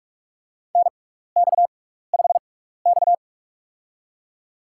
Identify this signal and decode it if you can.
Morse code, heard as a keyed tone, about 42 wpm, spaced as for 15 wpm, 700 Hz. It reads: NX5X
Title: USA Call Signs - 4 Characters - 6x Character Spacing - 45wpm